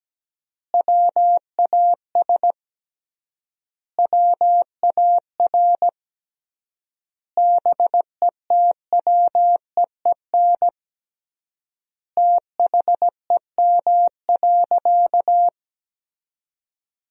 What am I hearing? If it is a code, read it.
Morse code, heard as a keyed tone, about 17 wpm, 700 Hz. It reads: WAS WAR BETWEEN THEM.
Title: 3x Word-Spacing - Sentences from Top 400 Words - 17wpm